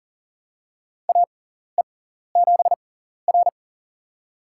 Morse code, heard as a keyed tone, about 40 wpm, spaced as for 14 wpm, 700 Hz. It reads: AE7R
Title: USA Call Signs - 6x Character Spacing - 40wpm